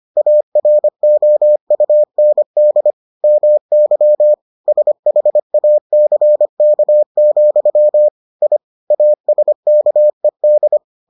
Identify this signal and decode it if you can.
Morse code, heard as a keyed tone, about 25 wpm, 600 Hz. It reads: AROUND MY SHACK, I ASKED